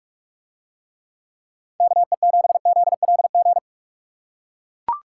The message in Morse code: KE7ZLG

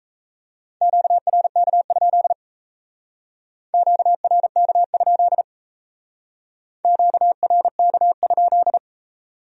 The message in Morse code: QRK? QRK? QRK?

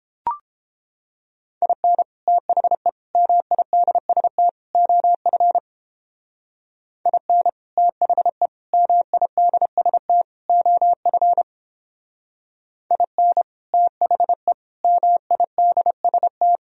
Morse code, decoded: IN THE MIDST OF IN THE MIDST OF IN THE MIDST

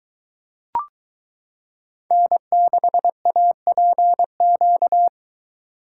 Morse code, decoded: N6APQ

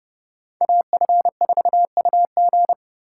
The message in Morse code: AF4UG

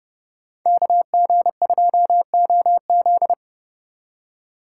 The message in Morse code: KG2OZ

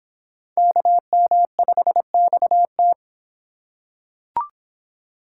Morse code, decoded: KM5XT

